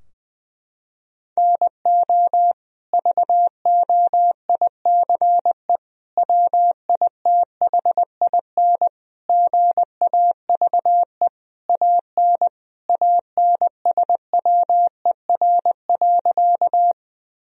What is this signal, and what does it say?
Morse code, heard as a keyed tone, about 20 wpm, 700 Hz. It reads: NO VOICE WITHIN GAVE AN ANSWER.
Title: Sentences from Top 1000 Words - Review Part 4 - 20wpm